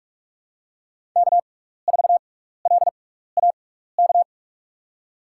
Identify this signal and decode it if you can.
Morse code, heard as a keyed tone, about 46 wpm, spaced as for 15 wpm, 700 Hz. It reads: K4LAK
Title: USA Call Signs - 6x Character Spacing - 45wpm